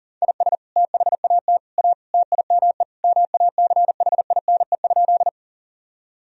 Morse code, heard as a keyed tone, about 40 wpm, 700 Hz. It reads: IS THAT A TIME MACHINE?
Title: Sentences from Top 1000 Words - Review Part 2 - 40wpm